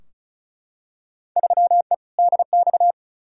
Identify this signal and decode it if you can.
Morse code, heard as a keyed tone, about 36 wpm, 700 Hz. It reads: 3E DX
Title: Field Day Exchanges Encore - 35wpm